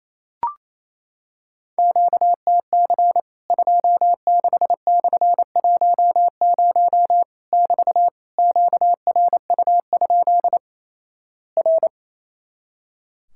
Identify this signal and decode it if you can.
Morse code, heard as a keyed tone, about 28 wpm, 700 Hz. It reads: QTC 26/10 <BT> QRU? R
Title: Worked All Europe DX Contest - QTC Exchange - 28wpm